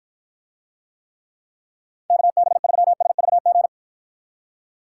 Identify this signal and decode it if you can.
Morse code, heard as a keyed tone, about 53 wpm, 700 Hz. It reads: KB3SVG